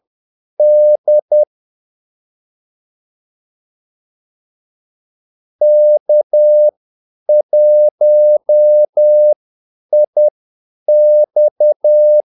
Morse code, heard as a keyed tone, about 10 wpm, spaced as for 5 wpm, 600 Hz. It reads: DK1IX
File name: Callsigns-10-8wpm